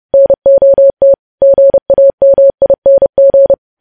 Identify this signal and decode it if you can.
Morse code, heard as a keyed tone, about 30 wpm, 550 Hz. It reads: NOT GAMING
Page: news headlines in morse code at 30 wpm